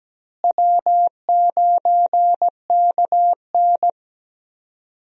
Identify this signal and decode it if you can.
Morse code, heard as a keyed tone, about 17 wpm, 700 Hz. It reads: W9KN
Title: USA Call Signs - 4 Characters - Repeated 2x - 17wpm